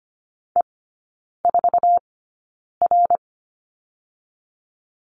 Morse code, heard as a keyed tone, about 25 wpm, spaced as for 9 wpm, 700 Hz. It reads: E4R
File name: Lesson 14e - Number 2 - Three Characters 25wpm